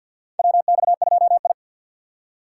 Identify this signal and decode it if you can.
Morse code, heard as a keyed tone, about 49 wpm, 700 Hz. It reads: WX2I
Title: USA Call Signs - 4 Characters - 6x Character Spacing - 50wpm